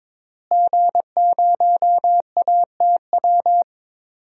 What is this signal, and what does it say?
Morse code, heard as a keyed tone, about 22 wpm, 700 Hz. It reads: G0ATW
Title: DXCC-100 Call Signs - 5 Characters - Repeated 2x - 22wpm